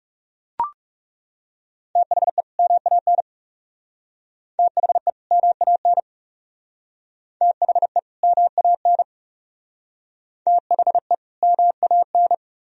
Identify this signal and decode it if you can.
Morse code, heard as a keyed tone, about 40 wpm, 700 Hz. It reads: THE MAN THE MAN THE MAN THE MAN